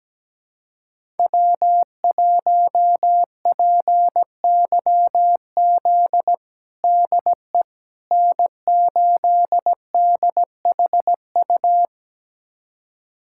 Morse code, heard as a keyed tone, about 17 wpm, 700 Hz. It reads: W1PYZ DE N8DHU
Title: USA Callsign DE Callsign - 5 Characters - 17wpm